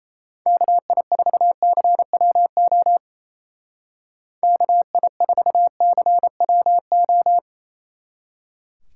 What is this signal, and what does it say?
Morse code, heard as a keyed tone, about 34 wpm, 700 Hz. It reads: KI4CWO KI4CWO
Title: Speed Racing - USA 2x3 Call Signs 28wpm